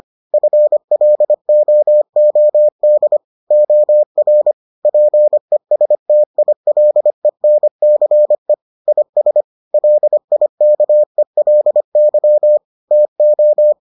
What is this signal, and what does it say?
Morse code, heard as a keyed tone, about 25 wpm, 600 Hz. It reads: FLOOD OR PESTILENCE IS LIKELY TO